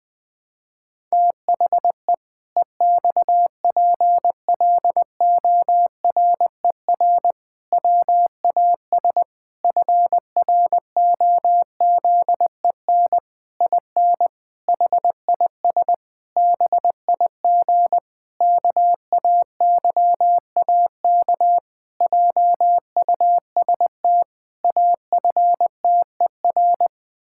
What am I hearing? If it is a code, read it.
Morse code, heard as a keyed tone, about 20 wpm, 700 Hz. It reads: THE EXPLORER WAS FROZEN IN HIS BIG KAYAK JUST AFTER
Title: Pangram Challenge v2 - 20wpm